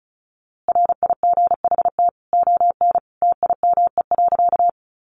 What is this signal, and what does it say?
Morse code, heard as a keyed tone, about 34 wpm, 700 Hz. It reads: RIGHT ON TIME.
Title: A Sound Of Thunder 35wpm